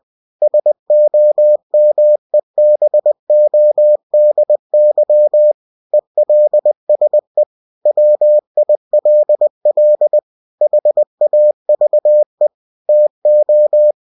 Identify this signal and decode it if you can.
Morse code, heard as a keyed tone, about 20 wpm, 600 Hz. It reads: SOMEBODY ELSE WILL HAVE TO
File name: cw-books / winnie-the-pooh / winnie-20wpm-page-0093